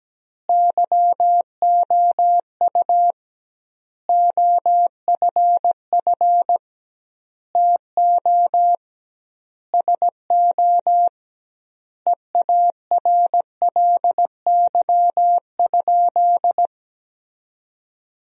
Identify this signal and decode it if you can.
Morse code, heard as a keyed tone, about 17 wpm, 700 Hz. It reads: YOU OFF TO SO EARLY?
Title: 2x Word-Spacing - Sentences from Top 400 Words - 17wpm